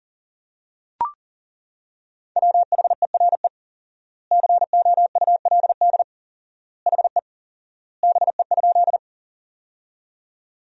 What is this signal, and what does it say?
Morse code, heard as a keyed tone, about 37 wpm, 700 Hz. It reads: WHERE COULD HE BE?
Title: 4x Word-Spacing - Sentences from Top 200 Words - 40wpm